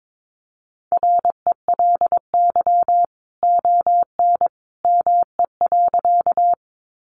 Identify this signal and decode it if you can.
Morse code, heard as a keyed tone, about 22 wpm, 700 Hz. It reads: RELY ON ME.
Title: Sentences from Top 1200 Words 22wpm